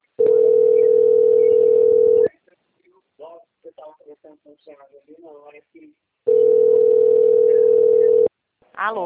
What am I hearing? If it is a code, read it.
Ringback tone (North America), 440 + 480 Hz, with speech.
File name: out- - - - - .5533